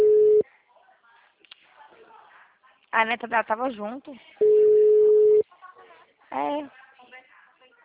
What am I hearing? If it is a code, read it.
Ringback tone (Europe, 425 Hz), with speech.